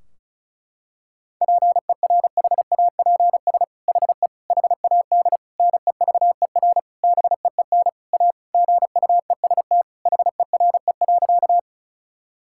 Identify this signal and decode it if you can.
Morse code, heard as a keyed tone, about 34 wpm, 700 Hz. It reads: PERHAPS HE HAD NEVER BEEN A GUEST HERE.
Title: Sentences from Top 1600 Words 35wpm